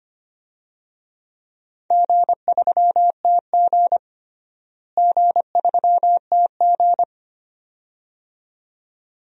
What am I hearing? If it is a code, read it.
Morse code, heard as a keyed tone, about 25 wpm, 700 Hz. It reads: G3TG G3TG
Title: DXCC-100 Call Signs - 4 Characters - Repeated 2x - 25wpm